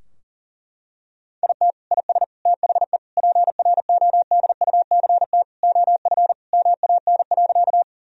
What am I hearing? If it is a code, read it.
Morse code, heard as a keyed tone, about 40 wpm, 700 Hz. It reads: IT IS THE PRODUCT OF MAN.
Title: Sentences from Top 1000 Words - Review Part 2 - 40wpm